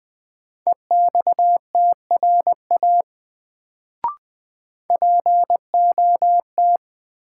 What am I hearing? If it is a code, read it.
Morse code, heard as a keyed tone, about 20 wpm, 700 Hz. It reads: EXTRA POT
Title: Sentences from Top 1000 Words - New Word Drill - 20wpm